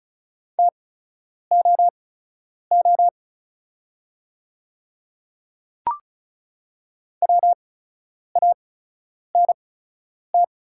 Morse code, heard as a keyed tone, about 36 wpm, spaced as for 10 wpm, 700 Hz. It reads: TOO WANT
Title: 8x Character Spacing - Top 300 Words - 35wpm